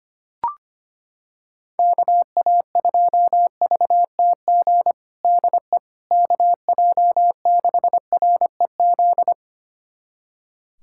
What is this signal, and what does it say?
Morse code, heard as a keyed tone, about 25 wpm, 700 Hz. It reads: KA2VTG DE KJ6REZ
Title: USA Callsign DE Callsign - 6 Characters - 25wpm